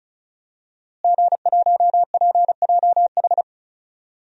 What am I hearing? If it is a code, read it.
Morse code, heard as a keyed tone, about 36 wpm, 700 Hz. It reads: G1PJH
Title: DXCC-100 Call Signs - 5 Characters - Repeated 3x - 35wpm